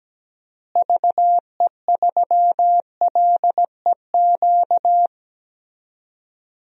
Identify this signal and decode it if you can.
Morse code, heard as a keyed tone, about 17 wpm, 700 Hz. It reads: VE3LEQ
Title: DXCC-100 Call Signs - 6 Characters - Repeated 3x - 17wpm